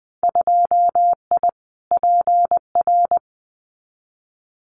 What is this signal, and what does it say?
Morse code, heard as a keyed tone, about 20 wpm, 700 Hz. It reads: 2I PR
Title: Winter Field Day Exchanges Encore 20wpm